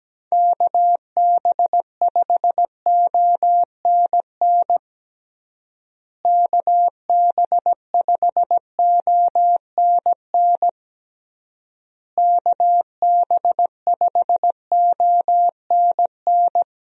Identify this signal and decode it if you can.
Morse code, heard as a keyed tone, about 17 wpm, 700 Hz. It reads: KB5ONN KB5ONN KB5ONN
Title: USA Call Signs - 6 Characters - Repeated 3x - 17wpm